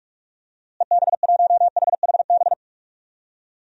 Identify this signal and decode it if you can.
Morse code, heard as a keyed tone, about 45 wpm, 700 Hz. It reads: EB1HHB
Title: DXCC-100 Call Signs - 6 Characters - Repeated 2x - 45wpm